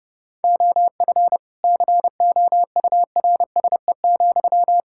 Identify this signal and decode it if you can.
Morse code, heard as a keyed tone, about 32 wpm, 700 Hz. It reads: OF COURSE,
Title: Sentences from Top 1600 Words 30wpm